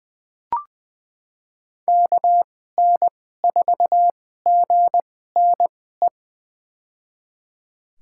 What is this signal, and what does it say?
Morse code, heard as a keyed tone, about 20 wpm, spaced as for 15 wpm, 700 Hz. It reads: KN4GNE
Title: USA Call Signs - 2x Character Spacing - 20wpm